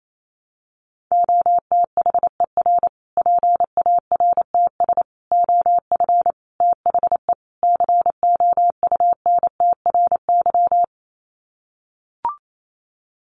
Morse code, heard as a keyed tone, about 28 wpm, 700 Hz. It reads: OTHER PARTS OF THE COUNTRY
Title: Sets of 5 Words Encore 28wpm